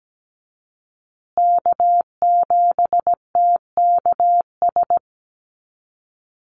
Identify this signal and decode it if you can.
Morse code, heard as a keyed tone, about 17 wpm, 700 Hz. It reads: K7TKS
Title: USA Call Signs Repeated 3x 17wpm